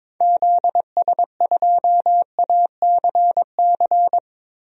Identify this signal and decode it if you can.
Morse code, heard as a keyed tone, about 22 wpm, 700 Hz. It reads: ZS2ACC
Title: DXCC-100 Call Signs - 6 Characters - Repeated 3x - 22wpm